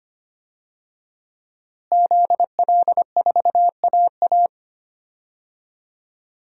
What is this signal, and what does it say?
Morse code, heard as a keyed tone, about 25 wpm, 700 Hz. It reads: ZL4AA
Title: DXCC-100 Call Signs - 5 Characters - Repeated 3x - 25wpm